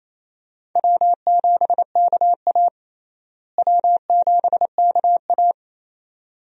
Morse code, heard as a keyed tone, about 28 wpm, 700 Hz. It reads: W7KA W7KA
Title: USA Call Signs - 4 Characters - Repeated 2x - 28wpm